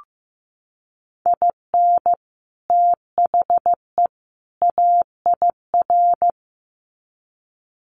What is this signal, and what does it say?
Morse code, heard as a keyed tone, about 15 wpm, 700 Hz. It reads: IN THE AIR
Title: Sets of 3 Words 15wpm